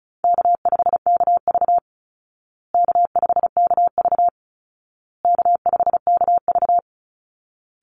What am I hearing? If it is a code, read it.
Morse code, heard as a keyed tone, about 34 wpm, 700 Hz. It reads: K5KV K5KV K5KV
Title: Easy Call Signs Repeated 3x - 35wpm